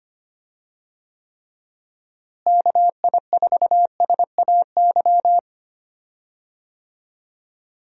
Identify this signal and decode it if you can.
Morse code, heard as a keyed tone, about 25 wpm, 700 Hz. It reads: KI4SAY
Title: USA Call Signs - 6 Characters - Repeated 2x - 25wpm